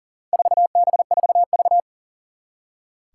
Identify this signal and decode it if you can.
Morse code, heard as a keyed tone, about 40 wpm, 700 Hz. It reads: 4B4V
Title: Four Challenging Letter-Number Combinations 40wpm